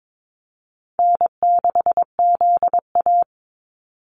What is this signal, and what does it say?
Morse code, heard as a keyed tone, about 22 wpm, 700 Hz. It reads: N6ZA
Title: NA SOTA Activator Callsigns 22wpm